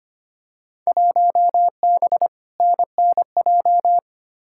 Morse code, heard as a keyed tone, about 25 wpm, 700 Hz. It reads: 1B NNJ
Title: Field Day Exchanges - 25wpm